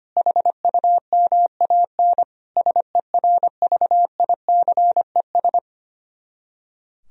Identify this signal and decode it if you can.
Morse code, heard as a keyed tone, about 24 wpm, 700 Hz. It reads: HUMAN SERVICES
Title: Speed-Racing - Sets of 5 Words 25wpm